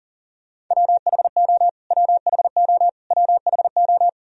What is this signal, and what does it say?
Morse code, heard as a keyed tone, about 40 wpm, 700 Hz. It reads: WHO WHO WHO
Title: Top 100 Words Repeated 40wpm